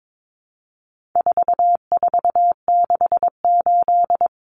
Morse code, heard as a keyed tone, about 22 wpm, 700 Hz. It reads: 4468
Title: Numbers - 4-Digit 22wpm